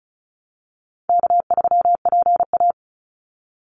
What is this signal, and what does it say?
Morse code, heard as a keyed tone, about 34 wpm, 700 Hz. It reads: K3PA
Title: Easy Call Signs Repeated 3x - 35wpm